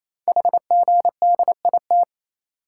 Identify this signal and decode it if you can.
Morse code, heard as a keyed tone, about 29 wpm, 700 Hz. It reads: HGDIT
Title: Code Groups - 5 Letters - 28wpm